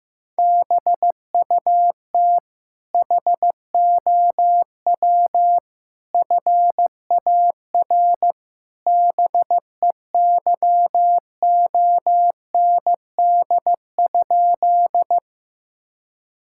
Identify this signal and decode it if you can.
Morse code, heard as a keyed tone, about 15 wpm, 700 Hz. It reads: BUT HOW FAR BEYOND?
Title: Sentences from Top 1000 Words - Review Part 6 - 15wpm